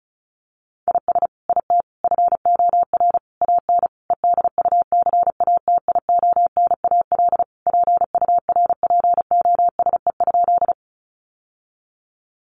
Morse code, heard as a keyed tone, about 34 wpm, 700 Hz. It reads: IS IT FOR AN EDUCATIONAL PURPOSE?
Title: Sentences from Top 1200 Words 35wpm